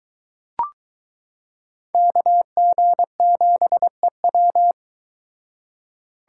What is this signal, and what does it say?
Morse code, heard as a keyed tone, about 22 wpm, 700 Hz. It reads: KG7EW